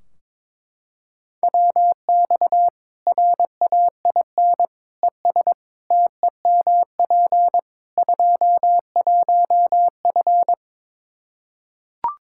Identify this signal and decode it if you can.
Morse code, heard as a keyed tone, about 22 wpm, 700 Hz. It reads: WX RAIN ES TEMP 21F
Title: Practice QSO - Exchange 2 - Info - 22wpm